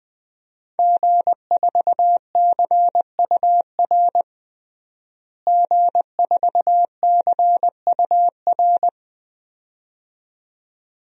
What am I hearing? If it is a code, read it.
Morse code, heard as a keyed tone, about 20 wpm, 700 Hz. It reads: G4CUR G4CUR
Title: DXCC-100 Call Signs - 5 Characters - Repeated 2x - 20wpm